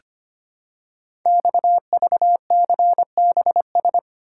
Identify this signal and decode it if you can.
Morse code, heard as a keyed tone, about 25 wpm, 700 Hz. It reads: XVCBS
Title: Code Groups - 5 Letters - 25wpm